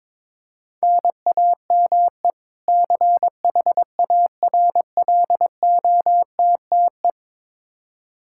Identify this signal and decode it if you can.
Morse code, heard as a keyed tone, about 21 wpm, 700 Hz. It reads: NAME CHARLOTTE
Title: Practice QSO - Exchange 1 - Beginning - 22wpm